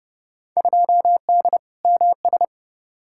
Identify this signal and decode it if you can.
Morse code, heard as a keyed tone, about 30 wpm, 700 Hz. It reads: 2D MS
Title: Field Day Exchanges Encore - 30wpm